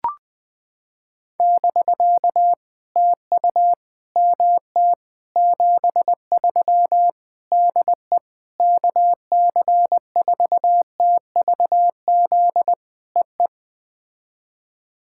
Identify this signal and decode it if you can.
Morse code, heard as a keyed tone, about 20 wpm, 700 Hz. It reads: <BK> TU MT 73 DE KC4TVZ EE
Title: POTA QSO Practice - 20wpm